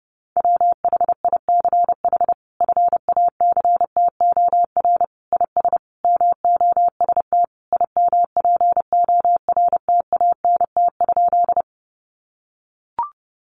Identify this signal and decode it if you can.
Morse code, heard as a keyed tone, about 30 wpm, 700 Hz. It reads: WHICH FACTOR IS MOST IMPORTANT?
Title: Sentences from Top 1300 Words 30wpm